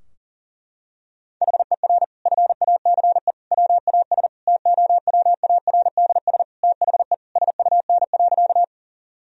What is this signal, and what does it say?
Morse code, heard as a keyed tone, about 40 wpm, 700 Hz. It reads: HER FACE WAS TOWARDS THE SUN.
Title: Sentences from Top 1000 Words - Review Part 2 - 40wpm